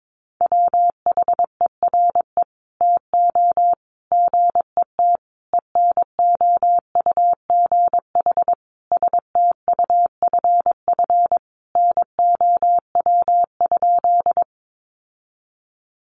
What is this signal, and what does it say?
Morse code, heard as a keyed tone, about 22 wpm, 700 Hz. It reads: WHERE TO GET ENOUGH STUFF NOW?
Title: Sentences from Top 1400 Words 22wpm